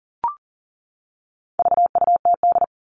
Morse code, heard as a keyed tone, about 40 wpm, 700 Hz. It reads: VUTD
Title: All Cut Numbers - 4-Digit 40wpm